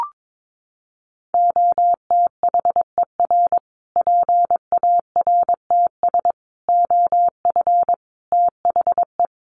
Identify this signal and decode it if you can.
Morse code, heard as a keyed tone, about 22 wpm, 700 Hz. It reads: OTHER PARTS OF THE
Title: Sets of 5 Words Encore 22wpm